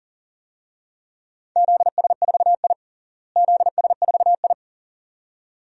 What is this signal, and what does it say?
Morse code, heard as a keyed tone, about 40 wpm, 700 Hz. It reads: ZS4I ZS4I
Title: DXCC-100 Call Signs - 4 Characters - Repeated 2x - 40wpm